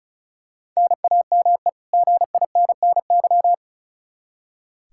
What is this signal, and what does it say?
Morse code, heard as a keyed tone, about 34 wpm, 700 Hz. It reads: NAME GINNY
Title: Practice QSO - Exchange 1 - Beginning - 35wpm